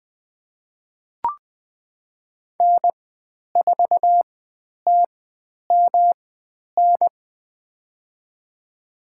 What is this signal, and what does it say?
Morse code, heard as a keyed tone, about 20 wpm, spaced as for 10 wpm, 700 Hz. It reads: N4TMN
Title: 109 Lesson 18g - Letter m - Call Signs 20wpm